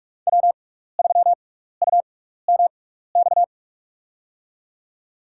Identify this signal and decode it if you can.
Morse code, heard as a keyed tone, about 44 wpm, spaced as for 15 wpm, 690 Hz. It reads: W3UMX
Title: USA Call Signs - 5 Characters - 6x Character Spacing - 45wpm